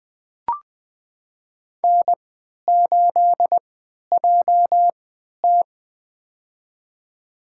Morse code, heard as a keyed tone, about 20 wpm, spaced as for 11 wpm, 700 Hz. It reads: N8JT
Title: USA Call Signs - 4 Characters - 3x Character Spacing - 20wpm